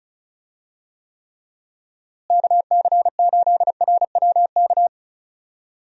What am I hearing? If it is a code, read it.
Morse code, heard as a keyed tone, about 36 wpm, 700 Hz. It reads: KC8RWK
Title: USA Call Signs - 6 Characters - Repeated 2x - 35wpm